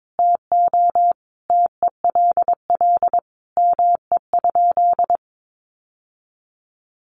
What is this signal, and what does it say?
Morse code, heard as a keyed tone, about 22 wpm, 700 Hz. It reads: TO TELL ME?
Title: Sentences from Top 600 Words 22wpm